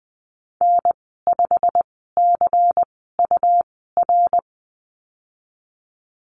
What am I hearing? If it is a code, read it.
Morse code, heard as a keyed tone, about 20 wpm, spaced as for 15 wpm, 700 Hz. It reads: N5CUR
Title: 161 Lesson 26g - Letter g - Call Signs 20wpm